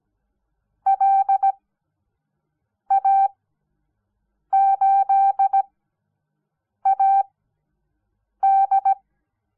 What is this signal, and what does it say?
Morse code, heard as a keyed tone, about 17 wpm, spaced as for 6 wpm, 780 Hz. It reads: LA8AD
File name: Onlinelehrgang / morsen / DJ4UF Morsen06